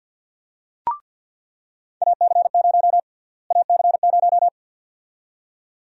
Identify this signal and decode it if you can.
Morse code, heard as a keyed tone, about 46 wpm, 700 Hz. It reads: AC0 AC0
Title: USA Call Sign Prefixes - Repeated 2x - 50wpm